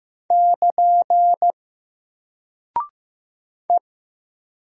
Morse code, heard as a keyed tone, about 15 wpm, 700 Hz. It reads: <KN> E
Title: Single Character (Letter:Number:Punctuation:Prosign) 15wpm